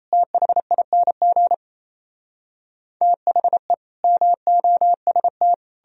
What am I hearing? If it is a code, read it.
Morse code, heard as a keyed tone, about 38 wpm, 700 Hz. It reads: THING THE MOST